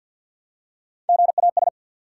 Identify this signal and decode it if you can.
Morse code, heard as a keyed tone, about 47 wpm, 700 Hz. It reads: GAS
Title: Three Letter Words - 50wpm